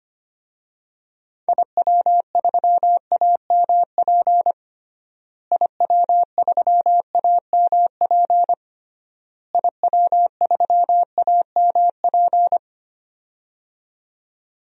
Morse code, heard as a keyed tone, about 25 wpm, 700 Hz. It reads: IW3AMP IW3AMP IW3AMP
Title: DXCC-100 Call Signs - 6 Characters - Repeated 3x - 25wpm